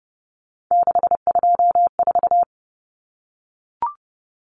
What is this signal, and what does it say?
Morse code, heard as a keyed tone, about 30 wpm, 700 Hz. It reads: 624
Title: Numbers - 3-Digit 30wpm